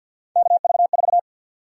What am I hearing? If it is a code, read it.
Morse code, heard as a keyed tone, about 49 wpm, 700 Hz. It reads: KV4
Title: USA Call Sign Prefixes - Repeated 2x - 50wpm